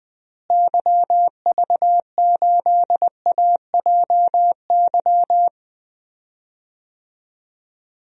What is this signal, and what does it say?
Morse code, heard as a keyed tone, about 20 wpm, 700 Hz. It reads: YV8AJY
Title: DXCC-100 Call Signs - 6 Characters - Repeated 3x - 20wpm